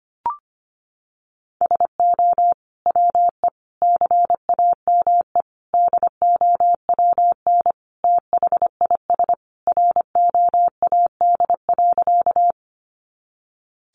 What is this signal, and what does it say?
Morse code, heard as a keyed tone, about 25 wpm, 700 Hz. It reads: SO WE CAME DOWN THIS ROAD.
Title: Sentences from Top 400 Words 25wpm